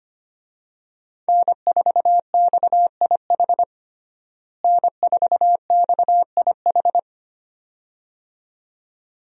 Morse code, heard as a keyed tone, about 25 wpm, 700 Hz. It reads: N4XIH N4XIH
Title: USA Call Signs - 5 Characters - Repeated 2x - 25wpm